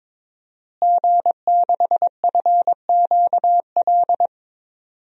Morse code, heard as a keyed tone, about 22 wpm, 700 Hz. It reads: G6FQL
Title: DXCC-100 Call Signs - 5 Characters - Repeated 2x - 22wpm